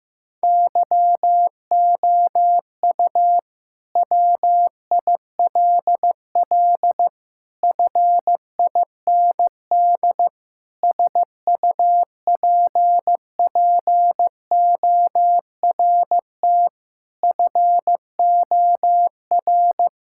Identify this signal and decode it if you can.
Morse code, heard as a keyed tone, about 15 wpm, 700 Hz. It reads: YOU WILL FIND SUPPORT FOR
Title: Sentences from Top 1000 Words - Review Part 5 - 15wpm